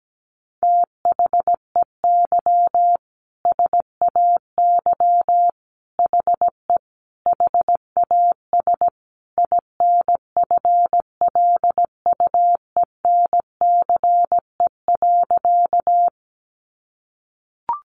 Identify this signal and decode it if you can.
Morse code, heard as a keyed tone, about 17 wpm, 700 Hz. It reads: THEY SAY HE HAS INFLUENCE.
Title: Sentences from Top 1100 Words 17wpm